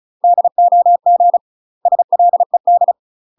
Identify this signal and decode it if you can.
Morse code, heard as a keyed tone, about 34 wpm, 700 Hz. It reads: DOG SLED
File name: UK MASTER QSOs TRAINING / 35WPM / UK MASTER QSOs 0021